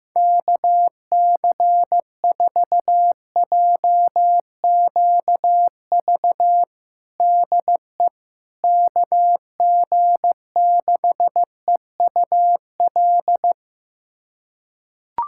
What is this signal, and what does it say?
Morse code, heard as a keyed tone, about 15 wpm, 700 Hz. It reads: KC4JQV DE KG6EUL
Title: USA Callsign DE Callsign - 6 Characters - 15wpm